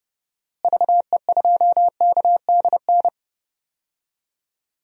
Morse code, heard as a keyed tone, about 30 wpm, 700 Hz. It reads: VE2KDN
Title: DXCC-100 Call Signs - 6 Characters - Repeated 3x - 30wpm